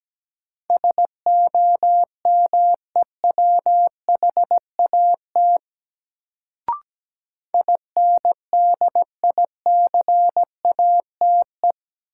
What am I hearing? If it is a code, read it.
Morse code, heard as a keyed tone, about 17 wpm, 700 Hz. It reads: SOMEWHAT INDICATE
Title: Sentences from Top 1300 Words - New Word Drill Alt - 17wpm